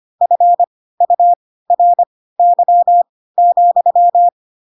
Morse code, heard as a keyed tone, about 25 wpm, spaced as for 16 wpm, 700 Hz. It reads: FURY,
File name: Pinocchio / Pinocchio ch4 10 25 ik1hge